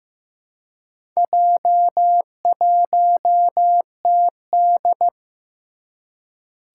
Morse code, heard as a keyed tone, about 15 wpm, 700 Hz. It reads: J1TD
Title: Four Challenging Letter-Number Combinations 15wpm